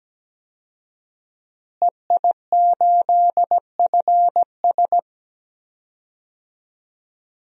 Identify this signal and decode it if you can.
Morse code, heard as a keyed tone, about 17 wpm, 700 Hz. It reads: EI8FS
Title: DXCC-100 Call Signs - 5 Characters - Repeated 3x - 17wpm